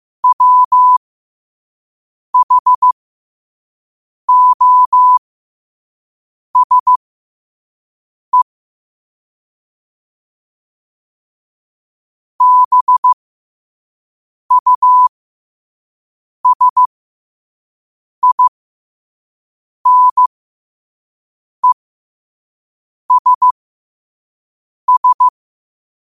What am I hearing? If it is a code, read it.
Morse code, heard as a keyed tone, about 15 wpm, spaced as for 5 wpm, 1000 Hz. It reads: WHOSE BUSINESS